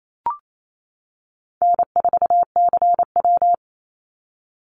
Morse code, heard as a keyed tone, about 28 wpm, 700 Hz. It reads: N4CW
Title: Call Signs - 28wpm